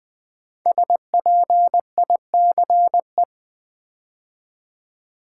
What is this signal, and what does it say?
Morse code, heard as a keyed tone, about 20 wpm, 700 Hz. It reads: SPICE
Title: 5 Letter Words - First 3 Letters Given - 20wpm